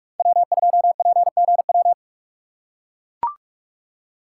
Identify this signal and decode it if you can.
Morse code, heard as a keyed tone, about 44 wpm, 700 Hz. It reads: W2PGW